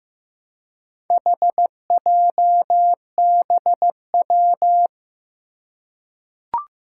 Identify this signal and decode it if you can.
Morse code, heard as a keyed tone, about 15 wpm, 700 Hz. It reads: HJBW
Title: Four Challenging Letter-Number Combinations 15wpm